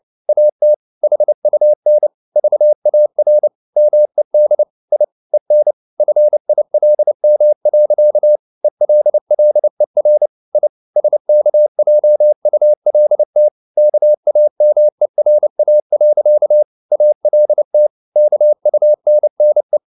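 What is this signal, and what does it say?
Morse code, heard as a keyed tone, about 29 wpm, 600 Hz. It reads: AT HUN VAR MED I EN FILM. ELLER I SKJULT KAMERA. ALT KUNNE
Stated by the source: Fast (QRQ) Morse at 29 wpm, from CW med Gnister.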